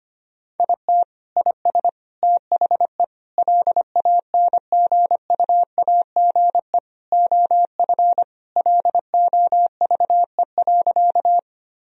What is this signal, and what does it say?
Morse code, heard as a keyed tone, about 25 wpm, 700 Hz. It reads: IT IS THE LANGUAGE OF LOVE.
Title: Sentences from Top 1000 Words - Review Part 3 - 25wpm